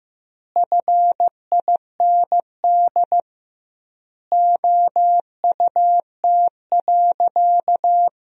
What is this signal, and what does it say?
Morse code, heard as a keyed tone, about 16 wpm, 700 Hz. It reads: FIND OUT.
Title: 2x Word-Spacing - Sentences from Top 200 Words - 15wpm